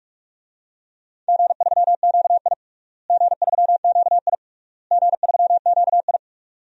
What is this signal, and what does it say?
Morse code, heard as a keyed tone, about 41 wpm, 700 Hz. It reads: G3QI G3QI G3QI
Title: DXCC-100 Call Signs - 4 Characters - Repeated 3x - 45wpm